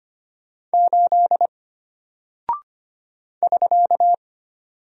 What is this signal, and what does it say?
Morse code, heard as a keyed tone, about 25 wpm, 700 Hz. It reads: 8 <SK>
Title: Single Character (Letter:Number:Punctuation:Prosign) Alt 25wpm